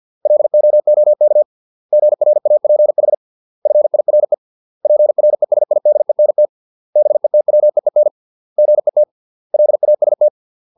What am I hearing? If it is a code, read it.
Morse code, heard as a keyed tone, about 50 wpm, 600 Hz. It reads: LOOK GRAPH FIRE PRESIDENT BETWEEN GET LAST